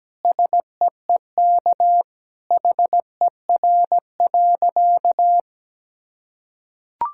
Morse code, heard as a keyed tone, about 17 wpm, 700 Hz. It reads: SEEK HER.
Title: Sentences from Top 1600 Words 17wpm